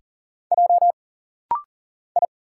Morse code, heard as a keyed tone, about 40 wpm, 700 Hz. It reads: J I
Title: Single Letters - Rapid-Fire 40wpm